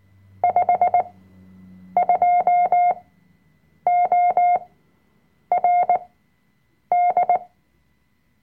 Morse code, heard as a keyed tone, about 19 wpm, spaced as for 7 wpm, 710 Hz. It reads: 52ORD
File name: Workshop Dateien / CW-Online-Kurs / Lektion 005